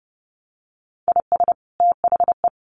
Morse code, heard as a keyed tone, about 30 wpm, 700 Hz. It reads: IS THE
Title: Sentences from Top 1200 Words 30wpm